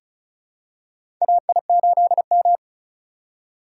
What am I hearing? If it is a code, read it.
Morse code, heard as a keyed tone, about 34 wpm, 700 Hz. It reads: AI8M
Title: USA Call Signs - 4 Characters - 6x Character Spacing - 35wpm